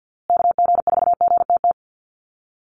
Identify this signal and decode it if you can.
Morse code, heard as a keyed tone, about 52 wpm, 700 Hz. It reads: KG4GTT
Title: USA Call Signs Repeated 2x 50wpm